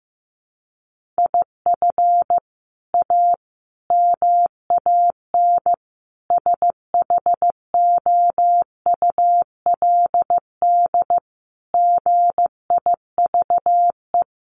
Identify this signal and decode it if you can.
Morse code, heard as a keyed tone, about 15 wpm, 700 Hz. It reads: IF A MAN SHOULD GIVE
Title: Sentences from Top 1500 Words 15wpm